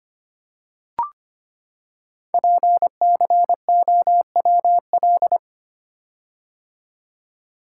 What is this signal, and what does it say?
Morse code, heard as a keyed tone, about 25 wpm, 700 Hz. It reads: PCOWL
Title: Code Groups - 5 Letters - 25wpm